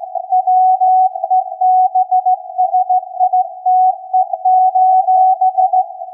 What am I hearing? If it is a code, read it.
Morse code, heard as a keyed tone, about 16 wpm, 740 Hz. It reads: WEBSITE:
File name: iz7slz / VLF / SAQ 16-11-2022